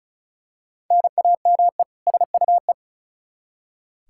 Morse code, heard as a keyed tone, about 36 wpm, 700 Hz. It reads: NAME SUE
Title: Practice QSO - Exchange 1 - Beginning - 35wpm